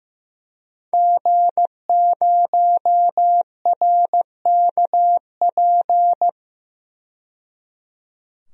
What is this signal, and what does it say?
Morse code, heard as a keyed tone, about 15 wpm, 700 Hz. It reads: G0RKP
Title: DXCC-100 Call Signs - 5 Characters - 15wpm